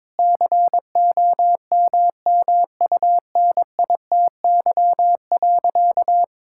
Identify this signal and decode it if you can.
Morse code, heard as a keyed tone, about 21 wpm, 700 Hz. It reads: COMMUNITY.
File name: practice / 22wpm / Sentences with 8-Word Phrases - 22wpm